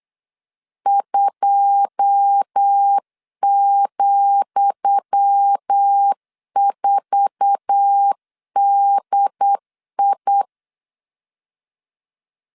Morse code, heard as a keyed tone, about 8 wpm, 800 Hz. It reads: 2,4DI
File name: LEZIONI / 10 SIGNS / 10D COMA 8wpm 3749